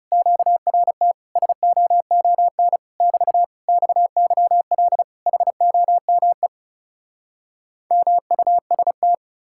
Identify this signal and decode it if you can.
Morse code, heard as a keyed tone, about 40 wpm, 700 Hz. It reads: QRT SOON <BT> XYL HOME MUST